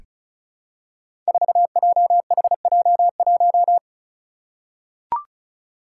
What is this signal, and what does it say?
Morse code, heard as a keyed tone, about 36 wpm, 700 Hz. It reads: 4JHJ1
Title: Five Challenging Letter-Number Combinations 35wpm